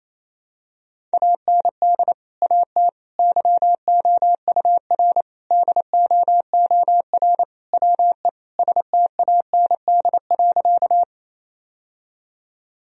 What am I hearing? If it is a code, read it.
Morse code, heard as a keyed tone, about 28 wpm, 700 Hz. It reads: AND AT YOUR DOOR WE STAND.
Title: Sentences from Top 1000 Words - Review Part 1 - 28wpm